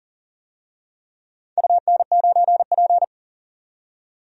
Morse code, heard as a keyed tone, about 40 wpm, 700 Hz. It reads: UN9P
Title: DXCC-100 Call Signs - 4 Characters - 40wpm